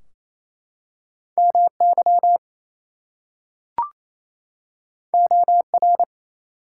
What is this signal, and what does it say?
Morse code, heard as a keyed tone, about 29 wpm, 700 Hz. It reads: MY OR
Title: Two Letter Words - 28wpm